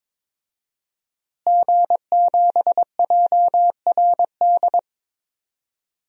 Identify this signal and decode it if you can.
Morse code, heard as a keyed tone, about 21 wpm, 700 Hz. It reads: G7JRD
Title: DXCC-100 Call Signs - 5 Characters - Repeated 3x - 22wpm